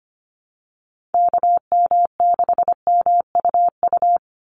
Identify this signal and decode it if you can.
Morse code, heard as a keyed tone, about 25 wpm, 700 Hz. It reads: KM6MUU
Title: USA Call Signs 25wpm